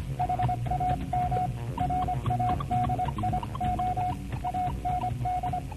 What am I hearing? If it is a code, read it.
Morse code, heard as a keyed tone, about 25 wpm, 720 Hz, with radio noise. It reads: HAM RADIO AND